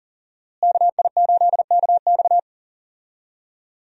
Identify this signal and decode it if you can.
Morse code, heard as a keyed tone, about 40 wpm, 700 Hz. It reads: KI8KX
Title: USA Call Signs - 5 Characters - Repeated 3x - 40wpm